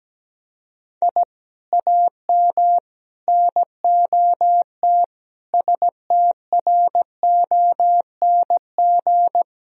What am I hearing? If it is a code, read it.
Morse code, heard as a keyed tone, about 17 wpm, 700 Hz. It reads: I AM NOT STRONG
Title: Sentences from Top 1000 Words - Review Part 3 - 17wpm